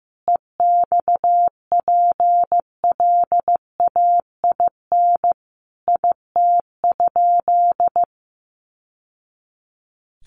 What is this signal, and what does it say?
Morse code, heard as a keyed tone, about 14 wpm, 700 Hz. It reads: EXPLAIN IT?
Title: Sentences from Top 1100 Words 15wpm